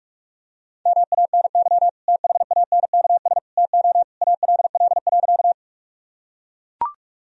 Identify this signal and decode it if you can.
Morse code, heard as a keyed tone, about 44 wpm, 700 Hz. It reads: MANY THANKS TO ALL.
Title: Sentences from Top 1000 Words - Review Part 3 - 45wpm